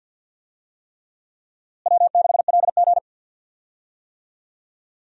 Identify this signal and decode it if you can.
Morse code, heard as a keyed tone, about 47 wpm, 700 Hz. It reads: W6LG
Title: USA Call Signs - 4 Characters - Repeated 3x - 50wpm